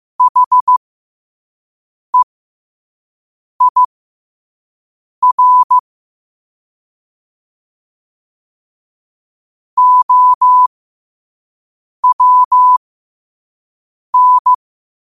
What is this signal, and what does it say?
Morse code, heard as a keyed tone, about 15 wpm, spaced as for 5 wpm, 1000 Hz. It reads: HEIR OWN
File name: qotdfiles / 5wpm